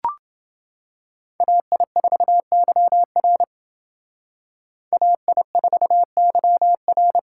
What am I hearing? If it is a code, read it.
Morse code, heard as a keyed tone, about 28 wpm, 700 Hz. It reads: AI4YR AI4YR